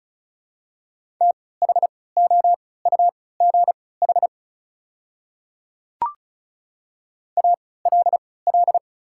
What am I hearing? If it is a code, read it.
Morse code, heard as a keyed tone, about 34 wpm, spaced as for 20 wpm, 700 Hz. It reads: THOUGH ALL
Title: 3x Character Spacing - Top 500 Words - 35wpm